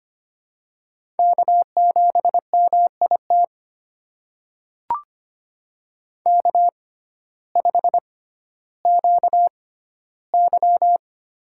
Morse code, heard as a keyed tone, about 25 wpm, spaced as for 14 wpm, 700 Hz. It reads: K7MIT K5QY
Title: USA Call Signs - 6x Character Spacing - 25wpm